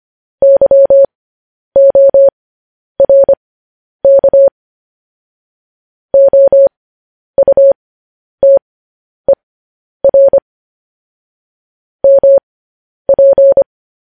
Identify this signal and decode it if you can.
Morse code, heard as a keyed tone, about 25 wpm, spaced as for 10 wpm, 550 Hz. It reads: YORK OUTER MP
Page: news-headlines-in-morse-code-at-10-wpm